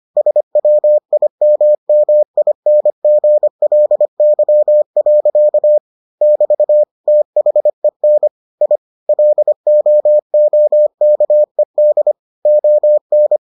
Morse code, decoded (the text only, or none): SWIMMINGLY. <BT> THEN I LOOKED ON